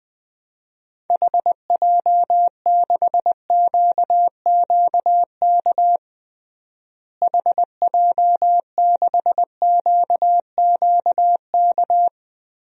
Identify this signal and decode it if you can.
Morse code, heard as a keyed tone, about 21 wpm, 700 Hz. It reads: HJ6QQK HJ6QQK
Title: DXCC-100 Call Signs - 6 Characters - Repeated 3x - 20wpm